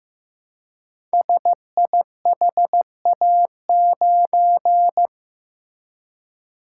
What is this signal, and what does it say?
Morse code, heard as a keyed tone, about 15 wpm, 700 Hz. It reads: SIHA9
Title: Code Groups - 5 Letter:Numbers - 15wpm